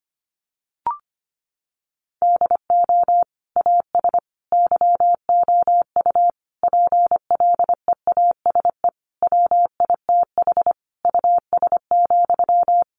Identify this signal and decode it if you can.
Morse code, heard as a keyed tone, about 25 wpm, 700 Hz. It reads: DO AS YOU PLEASE WITH US,
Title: Sentences from Top 1500 Words 25wpm